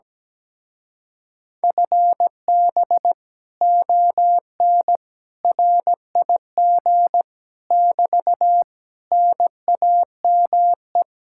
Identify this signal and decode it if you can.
Morse code, heard as a keyed tone, about 17 wpm, 700 Hz. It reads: FB ON RIG <BT> NAME
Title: Speed-Racing Common QSO Phrases - 17wpm